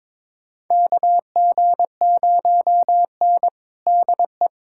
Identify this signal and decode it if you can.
Morse code, heard as a keyed tone, about 22 wpm, 700 Hz. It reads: KG0N DE
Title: USA Callsign DE Callsign - 4 Characters - 22wpm